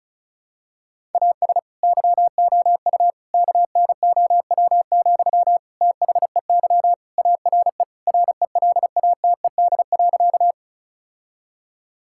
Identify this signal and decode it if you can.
Morse code, heard as a keyed tone, about 34 wpm, 700 Hz. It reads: AS YOU KNOW, THEY ARE RELATED.
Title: Sentences from Top 1000 Words - Review Part 5 - 35wpm